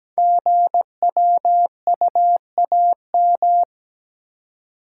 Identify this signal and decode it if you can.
Morse code, heard as a keyed tone, about 17 wpm, 700 Hz. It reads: GWUAM
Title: Code Groups - 5 Letters - 17wpm